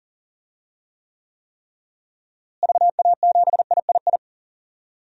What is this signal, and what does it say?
Morse code, heard as a keyed tone, about 40 wpm, 700 Hz. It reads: VA7III
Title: DXCC-100 Call Signs - 6 Characters - Repeated 3x - 40wpm